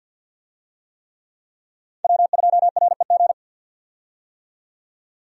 Morse code, heard as a keyed tone, about 50 wpm, 700 Hz. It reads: W2REG
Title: USA Call Signs - 5 Characters - 50wpm